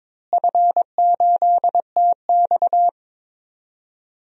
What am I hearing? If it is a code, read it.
Morse code, heard as a keyed tone, about 22 wpm, 700 Hz. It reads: F8TX
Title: DXCC-100 Call Signs - 4 Characters - Repeated 2x - 22wpm